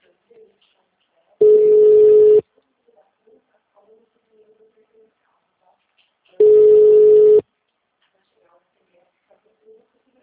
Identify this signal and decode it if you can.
Ringback tone (Europe, 425 Hz), with speech.
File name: out- -2035- - - .6